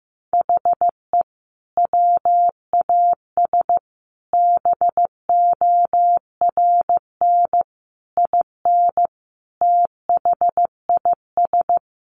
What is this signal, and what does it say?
Morse code, heard as a keyed tone, about 15 wpm, 700 Hz. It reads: HE WAS BORN IN THIS
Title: Sentences from Top 1000 Words 15wpm